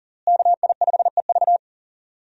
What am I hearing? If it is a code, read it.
Morse code, heard as a keyed tone, about 40 wpm, 700 Hz. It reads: KI5EV
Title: USA Call Signs - 5 Characters - Repeated 3x - 40wpm